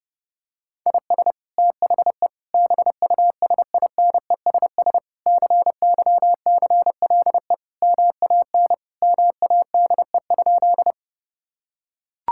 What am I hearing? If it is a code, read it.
Morse code, heard as a keyed tone, about 32 wpm, 700 Hz. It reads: IS THE BUSINESS CYCLE MAN MADE?
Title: Sentences from Top 1000 Words - Review Part 6 - 30wpm